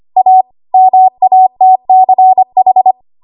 Morse code, heard as a keyed tone, about 25 wpm, 750 Hz. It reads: A MATCH
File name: Archive / 25 WPM / 25WPM